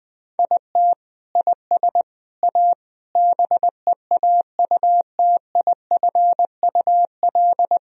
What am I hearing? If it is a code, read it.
Morse code, heard as a keyed tone, about 21 wpm, 700 Hz. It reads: IT IS A BEAUTIFUL
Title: Sentences from Top 1000 Words - Review Part 1 - 20wpm